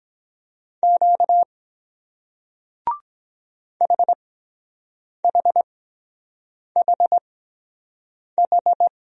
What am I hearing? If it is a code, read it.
Morse code, heard as a keyed tone, about 23 wpm, spaced as for 7 wpm, 700 Hz. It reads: Q HHHH